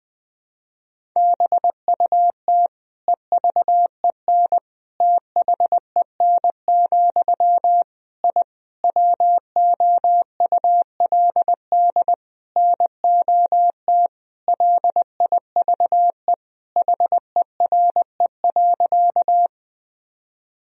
Morse code, heard as a keyed tone, about 20 wpm, 700 Hz. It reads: BUT EVEN THEN, I WOULD NOT LIVE HERE.
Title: Sentences from Top 1000 Words - Review Part 1 - 20wpm